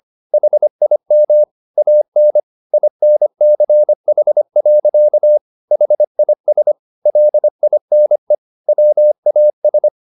Morse code, heard as a keyed tone, about 25 wpm, 600 Hz. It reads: HIM AN INCH. HIS LINE WAS
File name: the old man and the sea part 1 25wpm